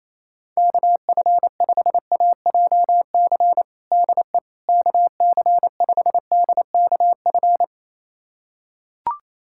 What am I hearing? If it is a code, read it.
Morse code, heard as a keyed tone, about 27 wpm, 700 Hz. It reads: KF5AJC DE KC5DKF
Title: USA Callsign DE Callsign - 6 Characters - 28wpm